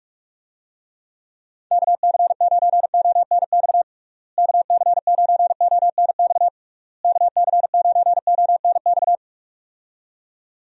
Morse code, heard as a keyed tone, about 46 wpm, 700 Hz. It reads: KC9ONX KC9ONX KC9ONX
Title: USA Call Signs - 6 Characters - Repeated 3x - 45wpm